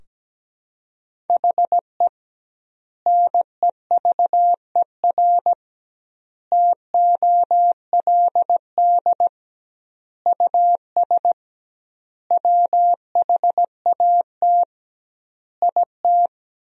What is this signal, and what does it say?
Morse code, heard as a keyed tone, about 17 wpm, 700 Hz. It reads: HE NEVER TOLD US WHAT IT
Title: 2x Word-Spacing - Sentences from Top 400 Words - 17wpm